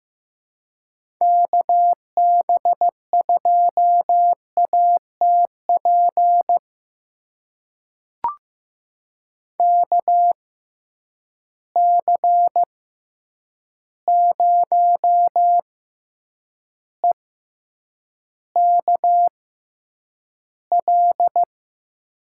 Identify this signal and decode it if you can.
Morse code, heard as a keyed tone, about 15 wpm, spaced as for 8 wpm, 700 Hz. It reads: KB2ATP KC0EKL
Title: USA Call Signs - 6 Characters - 6x Character Spacing - 15wpm